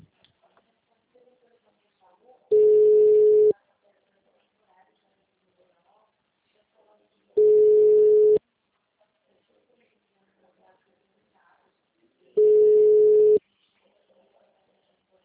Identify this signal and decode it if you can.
Ringback tone (Europe, 425 Hz).
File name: out- -2033- - - .7018